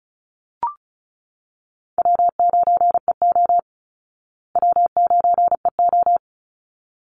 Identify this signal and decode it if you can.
Morse code, heard as a keyed tone, about 34 wpm, 700 Hz. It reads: W9EO W9EO
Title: Call Signs Repeated 3x - 35wpm